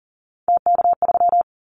Morse code, heard as a keyed tone, about 40 wpm, 700 Hz. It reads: TK3
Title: Callsign Prefixes for Top 100 DXCC - 40wpm